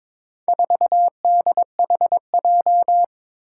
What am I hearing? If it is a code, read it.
Morse code, heard as a keyed tone, about 22 wpm, 700 Hz. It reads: 4DHJ
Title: Four Challenging Letter-Number Combinations 22wpm